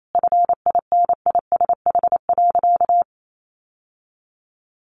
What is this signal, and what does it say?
Morse code, heard as a keyed tone, about 28 wpm, 700 Hz. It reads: FINISH.
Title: Sentences from Top 1200 Words 28wpm